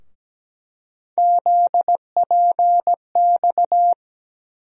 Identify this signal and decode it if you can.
Morse code, heard as a keyed tone, about 17 wpm, 700 Hz. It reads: ZPX
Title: USA Call Sign Postfixes - 3 Letters 17wpm